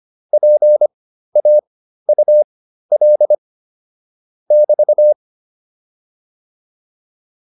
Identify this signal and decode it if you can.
Morse code, heard as a keyed tone, about 25 wpm, spaced as for 13 wpm, 600 Hz. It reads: PAUL <BT>